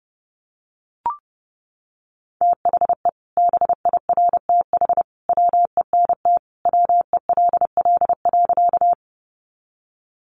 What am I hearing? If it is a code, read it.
Morse code, heard as a keyed tone, about 30 wpm, 700 Hz. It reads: THE BIRTH WENT WELL.
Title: Sentences from Top 800 Words 30wpm